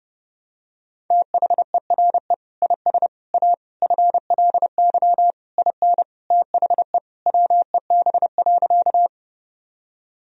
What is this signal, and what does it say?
Morse code, heard as a keyed tone, about 30 wpm, 700 Hz. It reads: THERE IS A FLY IN THE WEB.
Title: Sentences from Top 1000 Words - Review Part 4 - 30wpm